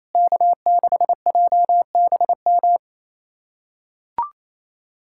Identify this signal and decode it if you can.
Morse code, heard as a keyed tone, about 28 wpm, 700 Hz. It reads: K6JBM
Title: USA Callsign DE Callsign - 5 Characters - 28wpm